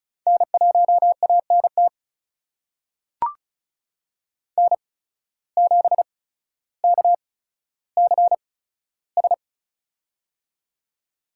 Morse code, heard as a keyed tone, about 36 wpm, spaced as for 15 wpm, 700 Hz. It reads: N1ANT N7KCS
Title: USA Call Signs - 5 Characters - 8x Character Spacing - 35wpm